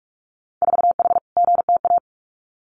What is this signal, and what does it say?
Morse code, heard as a keyed tone, about 46 wpm, 700 Hz. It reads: 4H GTA
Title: Winter Field Day Exchanges Encore 45wpm